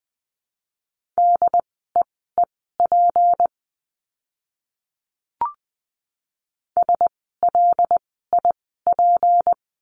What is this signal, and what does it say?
Morse code, heard as a keyed tone, about 20 wpm, spaced as for 14 wpm, 700 Hz. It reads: DEEP SLIP
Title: 153 Lesson 25f - Letter p - Words 20wpm